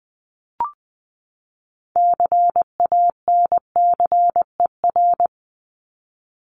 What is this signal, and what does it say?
Morse code, heard as a keyed tone, about 20 wpm, 700 Hz. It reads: CANCER
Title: Top 1251-1500 Words 20wpm